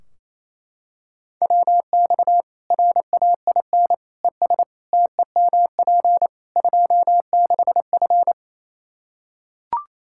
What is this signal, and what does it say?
Morse code, heard as a keyed tone, about 28 wpm, 700 Hz. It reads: WX RAIN ES TEMP 26F
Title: Practice QSO - Exchange 2 - Info - 28wpm